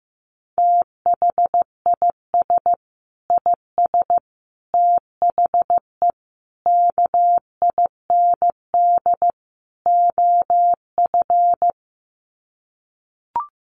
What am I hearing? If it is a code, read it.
Morse code, heard as a keyed tone, about 15 wpm, 700 Hz. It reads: THIS IS THE KIND OF
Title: Sets of 5 Words Encore 15wpm